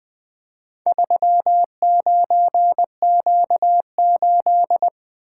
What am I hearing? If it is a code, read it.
Morse code, heard as a keyed tone, about 20 wpm, 700 Hz. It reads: 39Q8
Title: Four Letter-Number Combinations - 20wpm